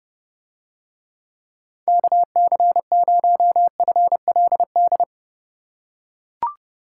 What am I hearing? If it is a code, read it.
Morse code, heard as a keyed tone, about 29 wpm, 700 Hz. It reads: KC0FLD